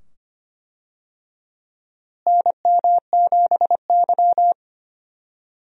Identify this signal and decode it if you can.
Morse code, heard as a keyed tone, about 25 wpm, 700 Hz. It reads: NM7Y
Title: USA Call Signs - 4 Characters - Repeated 2x - 25wpm